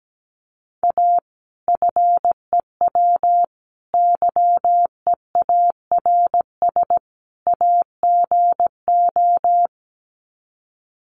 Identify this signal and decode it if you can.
Morse code, heard as a keyed tone, about 17 wpm, 700 Hz. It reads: A FEW YEARS AGO
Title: Sets of 4 Words 17wpm